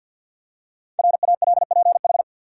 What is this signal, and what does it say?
Morse code, heard as a keyed tone, about 50 wpm, 700 Hz. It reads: RALPH
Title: USA Common Names - 50wpm